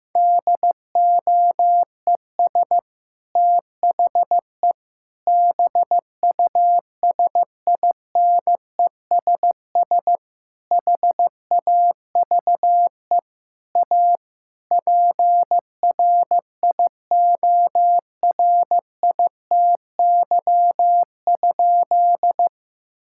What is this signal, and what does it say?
Morse code, heard as a keyed tone, about 15 wpm, 700 Hz. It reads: DOES THE BUSINESS HAVE A PRIORITY?
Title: Sentences from Top 1600 Words 15wpm